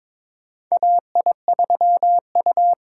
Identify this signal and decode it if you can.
Morse code, heard as a keyed tone, about 22 wpm, 700 Hz. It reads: AI3U
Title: USA Call Signs - 4 Characters - 4x Character Spacing - 22wpm